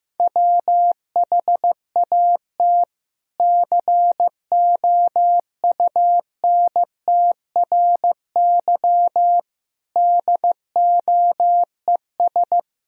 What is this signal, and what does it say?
Morse code, heard as a keyed tone, about 15 wpm, 700 Hz. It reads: WHAT COUNTRY DOES
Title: Sentences from Top 300 Words and Abbreviations 15wpm